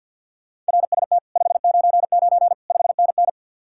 Morse code, heard as a keyed tone, about 67 wpm, spaced as for 58 wpm, 700 Hz. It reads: RST 599 5NN
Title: Speed-Racing Common QSO Phrases Encore - 50wpm